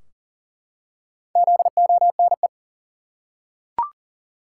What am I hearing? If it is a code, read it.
Morse code, heard as a keyed tone, about 40 wpm, 700 Hz. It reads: ZONE
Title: Four Letter Words - 40wpm